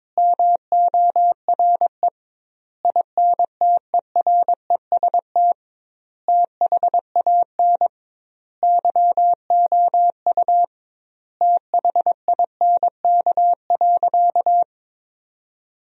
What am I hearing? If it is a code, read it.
Morse code, heard as a keyed tone, about 22 wpm, 700 Hz. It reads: MORE INTEREST THAN YOU THINK.
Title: 2x Word-Spacing - Sentences from Top 400 Words - 22wpm